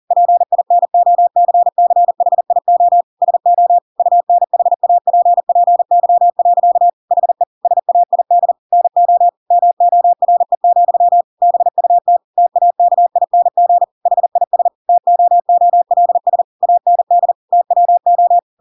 Morse code, heard as a keyed tone, about 40 wpm, 700 Hz. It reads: PINOCCHIO SO UNHAPPY. HE SAID NO MORE, BUT TAKING HIS TOOLS AND TWO
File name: Pinocchio / Pinocchio ch8 40 40 ik1hge